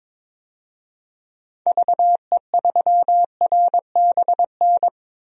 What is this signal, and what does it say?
Morse code, heard as a keyed tone, about 21 wpm, 700 Hz. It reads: VE3RBN
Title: DXCC-100 Call Signs - 6 Characters - 22wpm